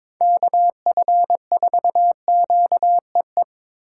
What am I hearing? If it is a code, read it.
Morse code, heard as a keyed tone, about 22 wpm, 700 Hz. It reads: KF4QEE
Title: USA Call Signs - 6 Characters - 4x Character Spacing - 22wpm